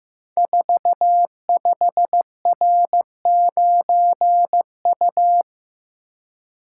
Morse code, heard as a keyed tone, about 15 wpm, 700 Hz. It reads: 45R9U
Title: Code Groups - 5 Letter:Numbers - 15wpm